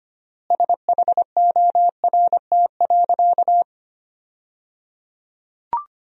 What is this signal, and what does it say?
Morse code, heard as a keyed tone, about 25 wpm, 700 Hz. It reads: SHORT.
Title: 3x Word-Spacing - Sentences from Top 500 Words - 25wpm